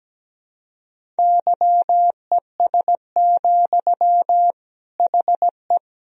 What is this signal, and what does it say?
Morse code, heard as a keyed tone, about 17 wpm, 700 Hz. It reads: YES, HE
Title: Sentences from Top 1000 Words - Review Part 1 - 17wpm